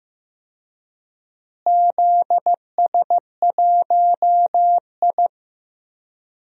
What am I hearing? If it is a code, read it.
Morse code, heard as a keyed tone, about 15 wpm, 700 Hz. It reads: ZS1I
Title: DXCC-100 Call Signs - 4 Characters - Repeated 3x - 15wpm